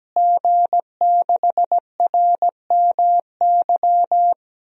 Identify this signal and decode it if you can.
Morse code, heard as a keyed tone, about 17 wpm, 700 Hz. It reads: G6RMY
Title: DXCC-100 Call Signs - 5 Characters - Repeated 2x - 17wpm